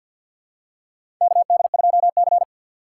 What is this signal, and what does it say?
Morse code, heard as a keyed tone, about 47 wpm, 700 Hz. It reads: KD2C
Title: USA Call Signs - 4 Characters - 3x Character Spacing - 50wpm